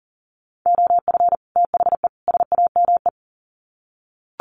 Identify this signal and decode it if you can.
Morse code, heard as a keyed tone, about 40 wpm, 700 Hz. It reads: OF THE SAME
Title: Sets of 3 Words 40wpm